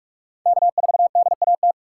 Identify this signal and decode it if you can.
Morse code, heard as a keyed tone, about 46 wpm, 700 Hz. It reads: K4DAT
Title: USA Call Signs - 2x Character Spacing - 45wpm